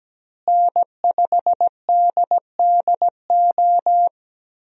Morse code, heard as a keyed tone, about 17 wpm, 700 Hz. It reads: N5DDO
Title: USA Call Signs - 5 Characters - Repeated 3x - 17wpm